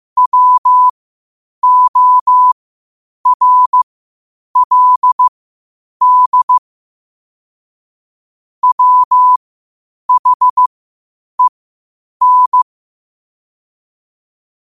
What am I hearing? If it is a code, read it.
Morse code, heard as a keyed tone, about 15 wpm, spaced as for 8 wpm, 1000 Hz. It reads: WORLD WHEN
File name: qotdfiles / 8wpm